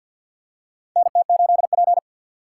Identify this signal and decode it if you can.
Morse code, heard as a keyed tone, about 45 wpm, 700 Hz. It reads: NT8P
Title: USA Call Signs - 4 Characters - 6x Character Spacing - 50wpm